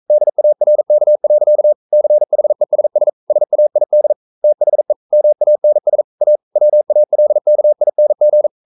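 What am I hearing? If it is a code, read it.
Morse code, heard as a keyed tone, about 42 wpm, 600 Hz. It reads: DARK. CHESS SAID THE MANS A WALKING
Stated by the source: Morse code practice at 41 wpm, from CW med Gnister.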